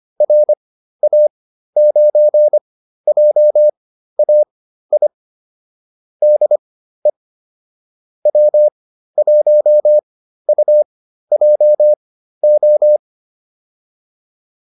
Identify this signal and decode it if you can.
Morse code, heard as a keyed tone, about 25 wpm, spaced as for 13 wpm, 600 Hz. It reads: RA9JAI DE W1UJO
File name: QSO401 13